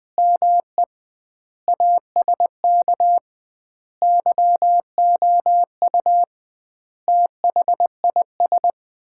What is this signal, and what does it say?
Morse code, heard as a keyed tone, about 20 wpm, 700 Hz. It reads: ME ASK YOU THIS
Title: 2x Word-Spacing - Sets of 5 Words - 20wpm